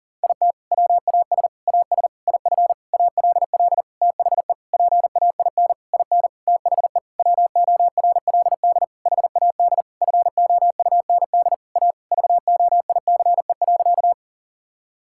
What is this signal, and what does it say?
Morse code, heard as a keyed tone, about 40 wpm, 700 Hz. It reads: IT WAS AS IF ALL THE PAIN IN THE WORLD HAD FOUND A VOICE.
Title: Sentences from Top 1600 Words 40wpm